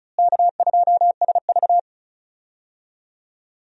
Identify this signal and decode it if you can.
Morse code, heard as a keyed tone, about 36 wpm, 700 Hz. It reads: K2SV
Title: USA Call Signs - 4 Characters - Repeated 3x - 35wpm